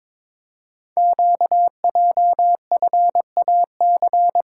Morse code, heard as a keyed tone, about 22 wpm, 700 Hz. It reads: QJFAC
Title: Code Groups - 5 Letters - 22wpm